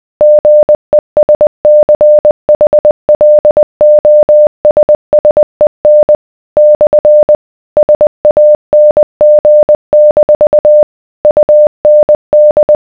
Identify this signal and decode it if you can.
Morse code, heard as a keyed tone, about 20 wpm, 600 Hz. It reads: GESCHLOSSEN / SANG- UND